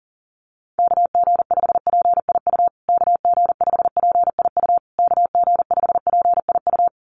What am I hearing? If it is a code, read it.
Morse code, heard as a keyed tone, about 40 wpm, 700 Hz. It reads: KG5PIU KG5PIU KG5PIU
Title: USA Call Signs Repeated 3x 40wpm